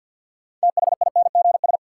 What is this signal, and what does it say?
Morse code, heard as a keyed tone, about 52 wpm, 700 Hz. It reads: THINGS
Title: Sentences from Top 1400 Words - New Word Drill Alt - 50wpm